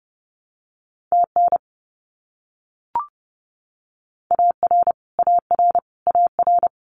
Morse code, heard as a keyed tone, about 30 wpm, 700 Hz. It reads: TN AR AR AR
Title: US States Repeated 3x 30wpm